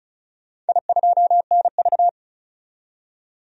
Morse code, heard as a keyed tone, about 36 wpm, 700 Hz. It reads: I2NV
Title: DXCC-100 Call Signs - 4 Characters - 35wpm